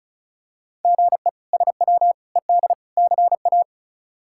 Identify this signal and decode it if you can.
Morse code, heard as a keyed tone, about 36 wpm, 700 Hz. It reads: GE SW ED CA
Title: K1USN SST QSO Practice - 35wpm